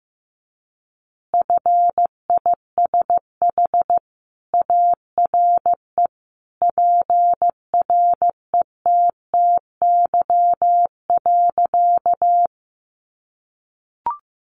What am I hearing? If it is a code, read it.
Morse code, heard as a keyed tone, about 14 wpm, 700 Hz. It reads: FISH ARE PRETTY.
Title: Additional Easy Phrases 15wpm